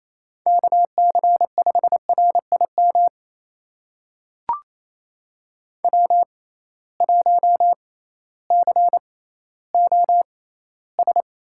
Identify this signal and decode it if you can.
Morse code, heard as a keyed tone, about 28 wpm, spaced as for 15 wpm, 700 Hz. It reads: KC5RIM W1COS
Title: USA Call Signs - 6x Character Spacing - 28wpm